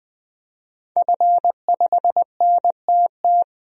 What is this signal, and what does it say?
Morse code, heard as a keyed tone, about 20 wpm, 700 Hz. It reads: F5NTT
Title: DXCC-100 Call Signs - 5 Characters - Repeated 2x - 20wpm